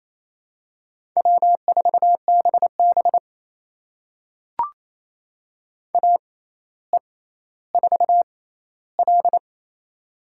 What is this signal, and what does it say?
Morse code, heard as a keyed tone, about 28 wpm, spaced as for 14 wpm, 700 Hz. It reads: W4BB AE4L
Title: USA Call Signs - 4 Characters - 6x Character Spacing - 28wpm